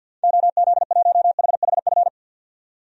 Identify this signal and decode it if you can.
Morse code, heard as a keyed tone, about 52 wpm, 700 Hz. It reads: OZ1HHF
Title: DXCC-100 Call Signs - 6 Characters - 50wpm